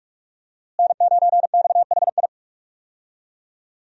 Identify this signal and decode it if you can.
Morse code, heard as a keyed tone, about 46 wpm, 700 Hz. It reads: N9XHI
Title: USA Call Signs - 5 Characters - Repeated 3x - 45wpm